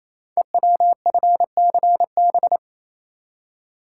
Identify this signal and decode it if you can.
Morse code, heard as a keyed tone, about 27 wpm, 700 Hz. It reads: EWFCB
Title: Code Groups - 5 Letters - 28wpm